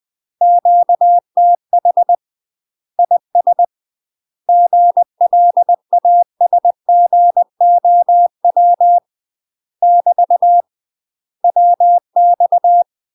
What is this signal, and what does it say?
Morse code, heard as a keyed tone, about 20 wpm, 700 Hz. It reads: QTH IS GLASGOW <BT> WX
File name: UK MASTER QSOs TRAINING / 20WPM / UK MASTER QSOs 0070